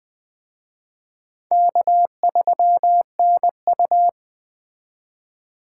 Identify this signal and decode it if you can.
Morse code, heard as a keyed tone, about 19 wpm, 700 Hz. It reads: K3NU